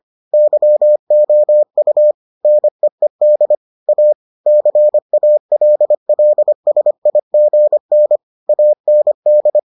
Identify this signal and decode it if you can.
Morse code, heard as a keyed tone, about 25 wpm, 600 Hz. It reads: YOU NEED A CALLSIGN AND